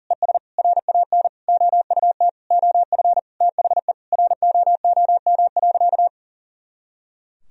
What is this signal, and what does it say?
Morse code, heard as a keyed tone, about 40 wpm, 700 Hz. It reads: ES RAN OUT OF THE ROOM.
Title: Sentences from Top 500 Words and Abbreviations 40wpm